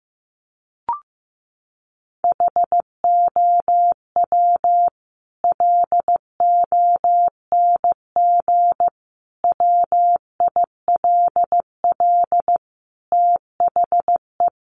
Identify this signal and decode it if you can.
Morse code, heard as a keyed tone, about 15 wpm, 700 Hz. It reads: HOW LONG WILL THE
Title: Sentences from Top 1200 Words 15wpm